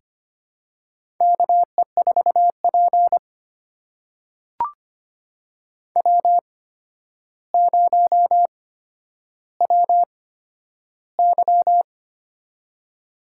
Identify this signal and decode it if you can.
Morse code, heard as a keyed tone, about 25 wpm, spaced as for 11 wpm, 700 Hz. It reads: KE4P W0WY
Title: USA Call Signs - 4 Characters - 8x Character Spacing - 25wpm